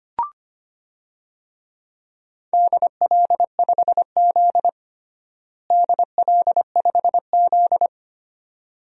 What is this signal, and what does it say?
Morse code, heard as a keyed tone, about 25 wpm, 700 Hz. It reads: DL5Z DL5Z
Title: DXCC-100 Call Signs - 4 Characters - Repeated 2x - 25wpm